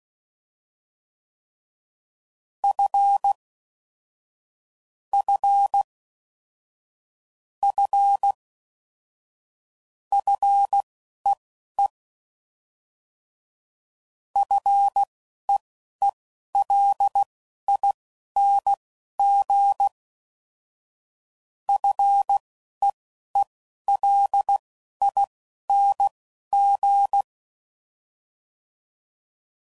Morse code, heard as a keyed tone, about 16 wpm, spaced as for 11 wpm, 800 Hz. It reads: F F F FEE FEELING FEELING
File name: Basic Sets / QST de WA7RSO / Combnations